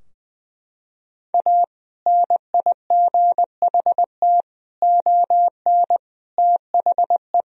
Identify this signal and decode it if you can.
Morse code, heard as a keyed tone, about 21 wpm, 700 Hz. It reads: A NIGHT ON THE
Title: Sentences from Top 1000 Words - Review Part 3 - 20wpm